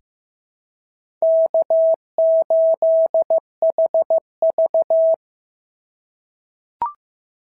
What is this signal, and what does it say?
Morse code, heard as a keyed tone, about 14 wpm, 690 Hz. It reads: K8HV
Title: SOTA QSO Practice 15wpm